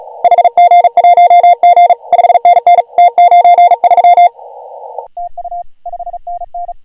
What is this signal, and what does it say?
Morse code, heard as a keyed tone, about 41 wpm, 690 Hz. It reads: HG1G 5NN T93 TU 5NN
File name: AUDIO2019 / HG1G CW